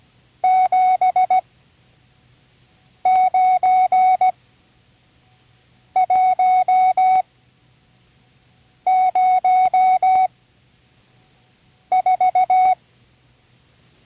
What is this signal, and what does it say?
Morse code, heard as a keyed tone, about 17 wpm, spaced as for 5 wpm, 740 Hz. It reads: 79104
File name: CW Learning Programs / Learning Morse Code (from Tune in the World) / 37 all numbers